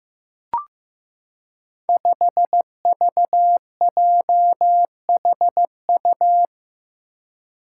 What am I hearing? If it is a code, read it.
Morse code, heard as a keyed tone, about 15 wpm, 700 Hz. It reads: 5VJHU
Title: Five Challenging Letter-Number Combinations 15wpm